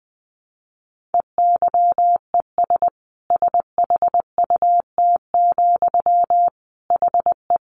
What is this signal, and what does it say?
Morse code, heard as a keyed tone, about 20 wpm, 700 Hz. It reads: EYES SHUT, HE
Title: A Sound Of Thunder 20wpm